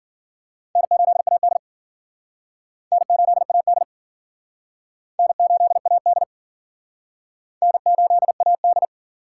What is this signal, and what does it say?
Morse code, heard as a keyed tone, about 51 wpm, 700 Hz. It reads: N8AD N8AD N8AD N8AD